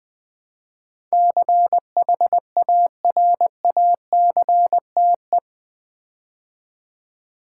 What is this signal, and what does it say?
Morse code, heard as a keyed tone, about 20 wpm, 700 Hz. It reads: CHARACTE
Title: Build A Word - 4 Letters At A Time - 12-Letter Words - 20wpm